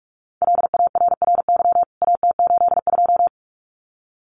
Morse code, heard as a keyed tone, about 44 wpm, 700 Hz. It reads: LARRY AT82
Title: CWT Exchange 45wpm